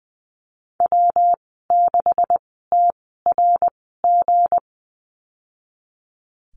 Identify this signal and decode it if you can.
Morse code, heard as a keyed tone, about 20 wpm, spaced as for 15 wpm, 700 Hz. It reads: W6TRG
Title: 179 Lesson 29g - Slash - Call Signs 20wpm